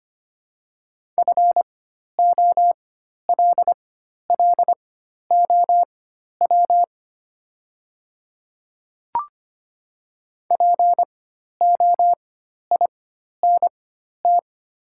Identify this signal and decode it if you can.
Morse code, heard as a keyed tone, about 25 wpm, spaced as for 12 wpm, 700 Hz. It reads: FOLLOW POINT
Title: 4x Character Spacing - Top 300 Words - 25wpm